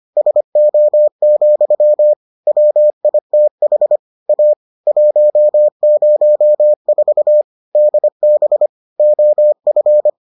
Morse code, heard as a keyed tone, about 25 wpm, 600 Hz. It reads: SO, WITH A 104 DB OF